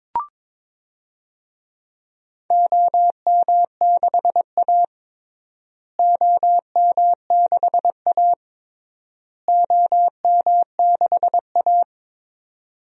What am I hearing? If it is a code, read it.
Morse code, heard as a keyed tone, about 22 wpm, 700 Hz. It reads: OM6A OM6A OM6A
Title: DXCC-100 Call Signs - 4 Characters - Repeated 3x - 22wpm